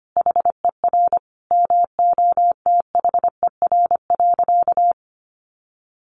Morse code, heard as a keyed tone, about 25 wpm, 700 Hz. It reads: HER MOTHER.
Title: Sentences from Top 900 Words 25wpm